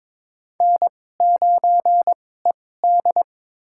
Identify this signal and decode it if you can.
Morse code, heard as a keyed tone, about 22 wpm, spaced as for 16 wpm, 700 Hz. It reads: N9ED
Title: USA Call Signs - 2x Character Spacing - 22wpm